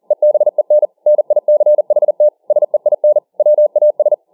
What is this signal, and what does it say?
Morse code, heard as a keyed tone, about 40 wpm, 600 Hz. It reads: EBEN NICHT SEIN WAS